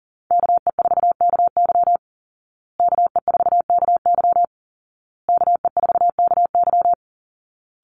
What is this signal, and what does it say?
Morse code, heard as a keyed tone, about 40 wpm, 700 Hz. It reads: KE4KY KE4KY KE4KY
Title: Call Signs Repeated 2x - 40wpm